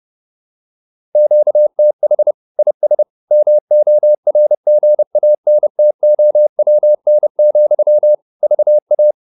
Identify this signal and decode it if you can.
Morse code, heard as a keyed tone, about 30 wpm, 600 Hz. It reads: QTH IS MORGANTOWN, VA
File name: QSO302 30